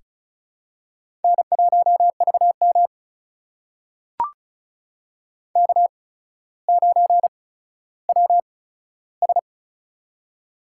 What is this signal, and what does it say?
Morse code, heard as a keyed tone, about 36 wpm, spaced as for 15 wpm, 700 Hz. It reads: N1VM K9WS
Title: USA Call Signs - 4 Characters - 8x Character Spacing - 35wpm